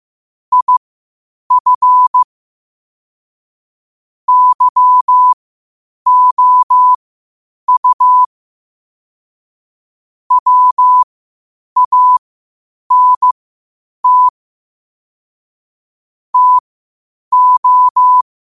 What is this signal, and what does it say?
Morse code, heard as a keyed tone, about 15 wpm, spaced as for 8 wpm, 1000 Hz. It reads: IF YOU WANT TO
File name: qotdfiles / 8wpm